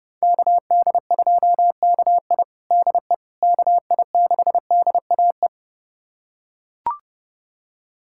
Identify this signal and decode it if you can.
Morse code, heard as a keyed tone, about 30 wpm, 700 Hz. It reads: KD2KI DE KI6DAE
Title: USA Callsign DE Callsign 30wpm